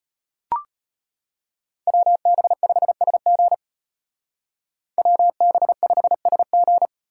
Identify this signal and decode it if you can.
Morse code, heard as a keyed tone, about 32 wpm, 700 Hz. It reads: WB5SG WB5SG